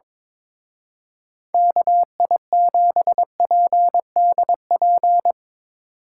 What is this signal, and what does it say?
Morse code, heard as a keyed tone, about 23 wpm, 700 Hz. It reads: KI7PDP
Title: Speed Racing - USA 2x3 Call Signs 22wpm